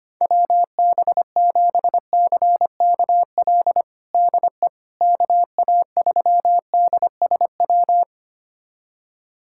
Morse code, heard as a keyed tone, about 25 wpm, 700 Hz. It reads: WB7CKL DE KA3DSW
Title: USA Callsign DE Callsign - 6 Characters - 25wpm